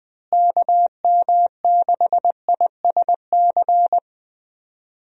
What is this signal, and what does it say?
Morse code, heard as a keyed tone, about 20 wpm, 700 Hz. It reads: KM6ISC
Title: USA Call Signs - 6 Characters - Repeated 3x - 20wpm